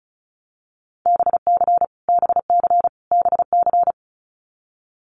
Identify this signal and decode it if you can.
Morse code, heard as a keyed tone, about 34 wpm, 700 Hz. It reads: BC BC BC
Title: Canadian Provinces and Territories - Repeated 3x - 35wpm